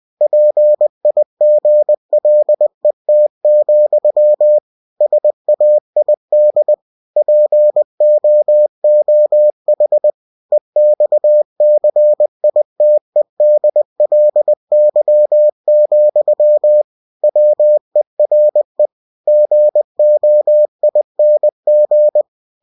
Morse code, heard as a keyed tone, about 20 wpm, 600 Hz. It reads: PIGLET, SAID POOH EXCITEDLY, WERE GOING
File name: cw-books / winnie-the-pooh / winnie-20wpm-page-0079